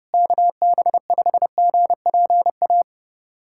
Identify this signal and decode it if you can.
Morse code, heard as a keyed tone, about 32 wpm, 700 Hz. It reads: KB5GPA
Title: USA Callsign DE Callsign - 6 Characters - 30wpm